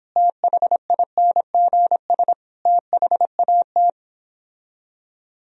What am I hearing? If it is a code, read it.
Morse code, heard as a keyed tone, about 31 wpm, 700 Hz. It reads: THINGS THAT